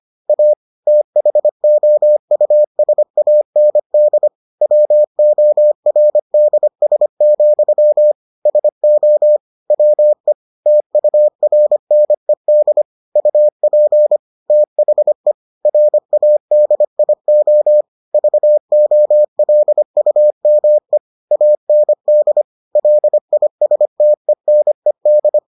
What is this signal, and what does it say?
Morse code, heard as a keyed tone, about 25 wpm, 600 Hz. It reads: A THOUSAND WORDS, SO WE TURNED UP THE RADIO VOLUME AND LISTENED